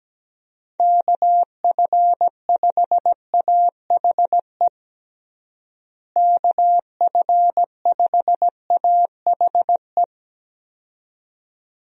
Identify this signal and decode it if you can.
Morse code, heard as a keyed tone, about 17 wpm, 700 Hz. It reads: KF5AHE KF5AHE
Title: USA Call Signs - 6 Characters - Repeated 2x - 17wpm